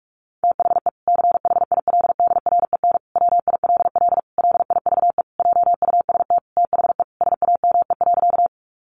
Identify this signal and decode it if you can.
Morse code, heard as a keyed tone, about 44 wpm, 700 Hz. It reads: THE CHILDREN WILL LIVE JUST THE SAME.
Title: Sentences from Top 300 Words 45wpm